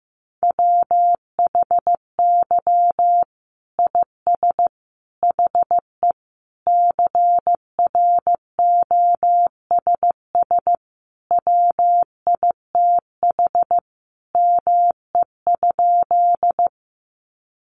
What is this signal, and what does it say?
Morse code, heard as a keyed tone, about 15 wpm, 700 Hz. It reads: WHY IS HE CROSS WITH ME?
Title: Sentences from Top 300 Words 15wpm